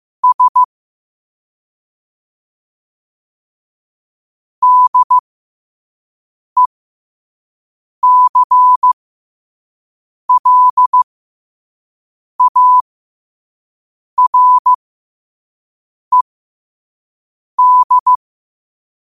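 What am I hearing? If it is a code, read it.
Morse code, heard as a keyed tone, about 15 wpm, spaced as for 5 wpm, 1000 Hz. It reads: S DECLARED
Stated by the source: Morse code sent at 5 wpm.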